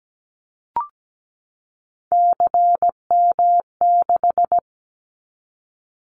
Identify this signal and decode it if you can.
Morse code, heard as a keyed tone, about 17 wpm, 700 Hz. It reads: CM6
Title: Callsign Prefixes for Top 100 DXCC - 17wpm